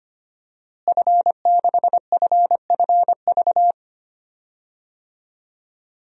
Morse code, heard as a keyed tone, about 25 wpm, 700 Hz. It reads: F6FFV
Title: DXCC-100 Call Signs - 5 Characters - Repeated 2x - 25wpm